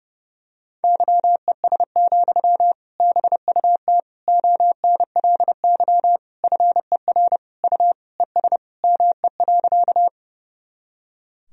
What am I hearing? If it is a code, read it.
Morse code, heard as a keyed tone, about 30 wpm, 700 Hz. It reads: YES, BUT ONLY FER U ES ME.
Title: Sentences from Top 500 Words and Abbreviations 30wpm